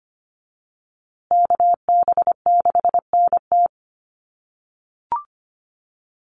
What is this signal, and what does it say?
Morse code, heard as a keyed tone, about 25 wpm, 700 Hz. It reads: KB6NT
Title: USA Call Signs Repeated 3x 25wpm